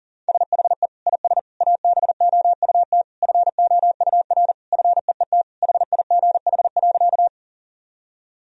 Morse code, heard as a keyed tone, about 40 wpm, 700 Hz. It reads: SHE IS ABOUT FOUR FEET HIGH.
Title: Sentences from Top 1000 Words - Review Part 1 - 40wpm